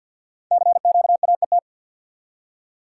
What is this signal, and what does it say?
Morse code, heard as a keyed tone, about 48 wpm, 700 Hz. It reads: CQAET
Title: Code Groups - 5 Letter:Numbers - 50wpm